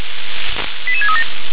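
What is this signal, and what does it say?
Five-tone selective call (ZVEI), heard as quick runs of five tones, tones of about 80 ms. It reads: 88637
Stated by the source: Five-tone ZVEI selective call received on an SDR.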